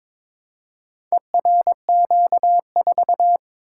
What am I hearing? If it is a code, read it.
Morse code, heard as a keyed tone, about 22 wpm, 700 Hz. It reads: ERQ4
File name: practice / 22wpm / Four Letter-Number Combinations - 22wpm